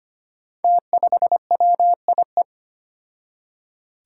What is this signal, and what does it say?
Morse code, heard as a keyed tone, about 25 wpm, 700 Hz. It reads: T5WIE
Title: Five Challenging Letter-Number Combinations 25wpm